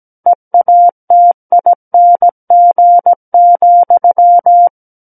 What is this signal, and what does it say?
Morse code, heard as a keyed tone, about 17 wpm, 700 Hz. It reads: EATING,
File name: Book One / 08-Chapter 8 - 17 WPM